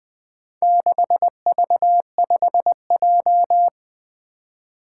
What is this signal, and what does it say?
Morse code, heard as a keyed tone, about 21 wpm, 700 Hz. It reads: 6V5J
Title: Four Challenging Letter-Number Combinations 20wpm